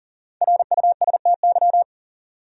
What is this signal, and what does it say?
Morse code, heard as a keyed tone, about 38 wpm, 700 Hz. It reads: RUSTY
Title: USA Common Names - 40wpm